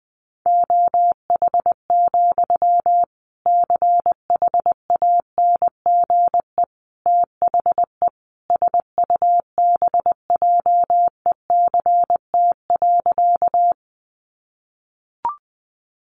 Morse code, heard as a keyed tone, about 19 wpm, 700 Hz. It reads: OH, CHANGE THE SUBJECT.
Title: Sentences from Top 800 Words 20wpm